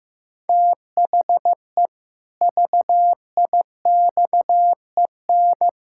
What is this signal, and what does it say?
Morse code, heard as a keyed tone, about 15 wpm, 700 Hz. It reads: THE VIXEN
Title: Pangram Challenge v2 - 15wpm